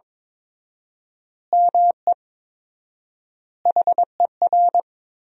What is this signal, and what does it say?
Morse code, heard as a keyed tone, about 22 wpm, 700 Hz. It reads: ME HER
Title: 4x Word-Spacing - Sentences from Top 500 Words - 22wpm